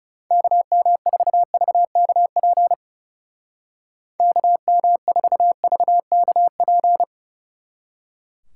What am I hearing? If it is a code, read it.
Morse code, heard as a keyed tone, about 36 wpm, 700 Hz. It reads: KM4VKP KM4VKP
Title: Speed Racing - USA 2x3 Call Signs 30wpm